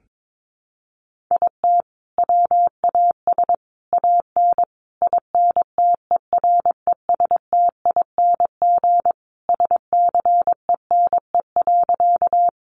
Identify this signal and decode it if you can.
Morse code, heard as a keyed tone, about 22 wpm, 700 Hz. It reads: IT WAS AN INTERESTING SCENE.
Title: Sentences from Top 1100 Words 22wpm